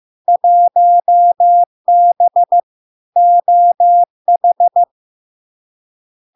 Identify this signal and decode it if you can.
Morse code, heard as a keyed tone, about 15 wpm, 700 Hz. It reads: 1B OH
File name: LARK-FD-2022-Log-in-CW-15wpm-700-Hz